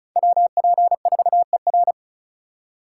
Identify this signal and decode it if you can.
Morse code, heard as a keyed tone, about 46 wpm, 700 Hz. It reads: WP4ER